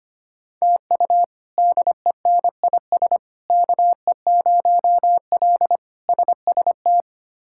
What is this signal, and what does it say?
Morse code, heard as a keyed tone, about 25 wpm, 700 Hz. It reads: TU DENIS KE0L SST
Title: K1USN SST QSO Practice - 25wpm